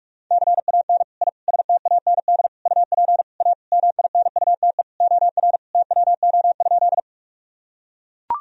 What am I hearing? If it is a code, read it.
Morse code, heard as a keyed tone, about 41 wpm, 700 Hz. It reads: CAN I STAND UP A MINUTE OR TWO?
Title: Sentences from Top 1000 Words - Review Part 1 - 45wpm